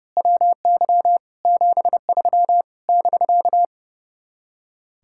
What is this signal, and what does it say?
Morse code, heard as a keyed tone, about 28 wpm, 700 Hz. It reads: WY 73 <BK>
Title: Speed-Racing Common QSO Phrases Encore - 30wpm